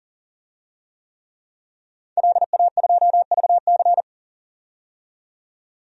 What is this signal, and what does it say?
Morse code, heard as a keyed tone, about 40 wpm, 700 Hz. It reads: LA2VC
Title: DXCC-100 Call Signs - 5 Characters - 40wpm